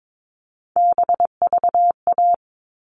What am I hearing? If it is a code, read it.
Morse code, heard as a keyed tone, about 22 wpm, 700 Hz. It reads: BVA
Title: All Cut Numbers - 3-Digit 22wpm